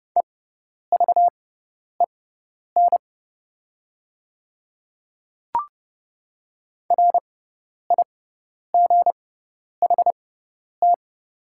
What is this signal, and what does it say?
Morse code, heard as a keyed tone, about 29 wpm, spaced as for 10 wpm, 700 Hz. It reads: EVEN RIGHT
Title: 6x Character Spacing - Top 300 Words - 30wpm